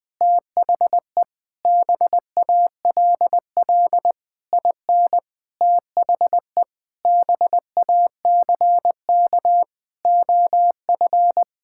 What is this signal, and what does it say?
Morse code, heard as a keyed tone, about 21 wpm, 700 Hz. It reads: THE BALL IN THE BACK OF
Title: Sets of 9 Words - 20wpm